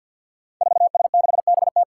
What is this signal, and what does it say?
Morse code, heard as a keyed tone, about 52 wpm, 700 Hz. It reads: 4S6BT
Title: Five Challenging Letter-Number Combinations 50wpm